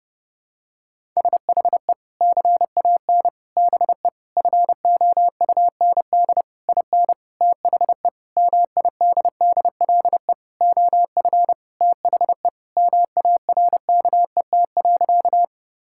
Morse code, heard as a keyed tone, about 30 wpm, 700 Hz. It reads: SHE CAN BE FOUND IN THE MIDDLE OF THE MARKET.
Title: Sentences from Top 1000 Words - Review Part 6 - 30wpm